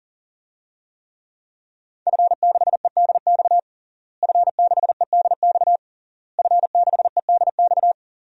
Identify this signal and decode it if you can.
Morse code, heard as a keyed tone, about 40 wpm, 700 Hz. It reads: F6EDX F6EDX F6EDX
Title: DXCC-100 Call Signs - 5 Characters - Repeated 3x - 40wpm